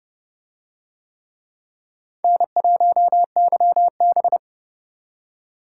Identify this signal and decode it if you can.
Morse code, heard as a keyed tone, about 28 wpm, 700 Hz. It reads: N1YB